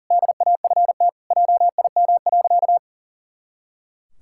Morse code, heard as a keyed tone, about 40 wpm, 700 Hz. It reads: DAFT JIM.
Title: Pangram Challenge v2 - 40wpm